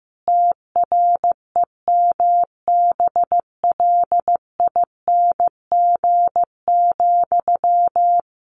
Morse code, heard as a keyed tone, about 15 wpm, 700 Hz. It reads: TREMBLING,
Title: A Sound Of Thunder 15wpm